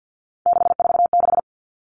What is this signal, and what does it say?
Morse code, heard as a keyed tone, about 50 wpm, 700 Hz. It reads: 646
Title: Numbers - 3-Digit 50wpm